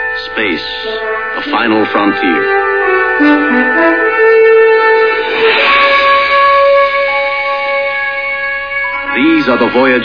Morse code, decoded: none